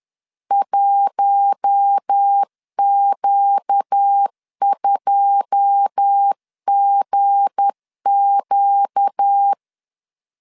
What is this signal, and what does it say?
Morse code, heard as a keyed tone, about 10 wpm, 800 Hz. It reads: 1Q2GQ